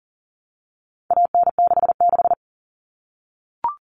Morse code, heard as a keyed tone, about 37 wpm, 700 Hz. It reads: AN66
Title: All Cut Numbers - 4-Digit 40wpm